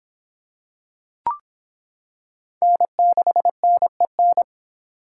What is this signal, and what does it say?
Morse code, heard as a keyed tone, about 26 wpm, 700 Hz. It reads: N6NEN